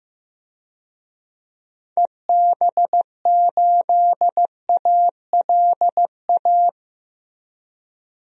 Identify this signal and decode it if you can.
Morse code, heard as a keyed tone, about 15 wpm, 700 Hz. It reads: EB8ALA
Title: DXCC-100 Call Signs - 6 Characters - 15wpm